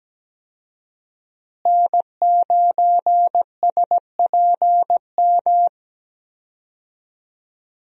Morse code, heard as a keyed tone, about 17 wpm, 700 Hz. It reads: N9SPM
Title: USA Call Signs - 5 Characters - Repeated 2x - 17wpm